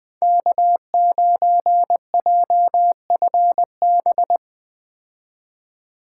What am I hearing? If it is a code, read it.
Morse code, heard as a keyed tone, about 20 wpm, 700 Hz. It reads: K9JFB
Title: 2023 POTA Activator Call Signs - 20wpm